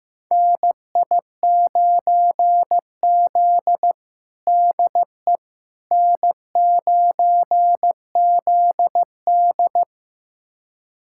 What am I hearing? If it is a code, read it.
Morse code, heard as a keyed tone, about 15 wpm, 700 Hz. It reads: NI9Z DE N9ZD
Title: USA Callsign DE Callsign - 4 Characters - 15wpm